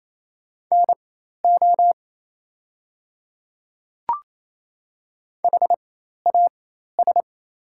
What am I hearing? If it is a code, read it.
Morse code, heard as a keyed tone, about 27 wpm, spaced as for 13 wpm, 700 Hz. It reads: NO HAS
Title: 4x Character Spacing - Top 100 Words - 28wpm